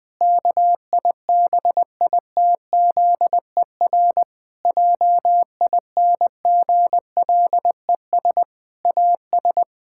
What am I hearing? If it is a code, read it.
Morse code, heard as a keyed tone, about 20 wpm, 700 Hz. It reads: KIBITZER JINGLES AS
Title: Pangram Challenge v2 - 20wpm